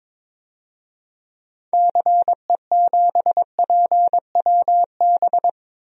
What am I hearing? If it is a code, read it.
Morse code, heard as a keyed tone, about 22 wpm, 700 Hz. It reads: CE7PWB
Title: DXCC-100 Call Signs - 6 Characters - Repeated 2x - 22wpm